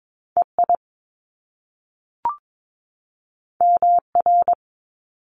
Morse code, heard as a keyed tone, about 22 wpm, 700 Hz. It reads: EI MR
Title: Top Two Letter Combinations in Words 22wpm